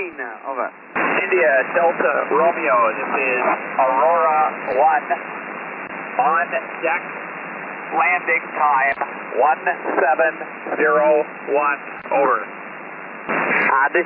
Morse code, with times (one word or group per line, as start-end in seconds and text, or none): none